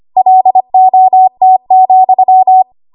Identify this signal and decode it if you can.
Morse code, heard as a keyed tone, about 25 wpm, 750 Hz. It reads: LOT,
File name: Archive / 25 WPM / 25WPM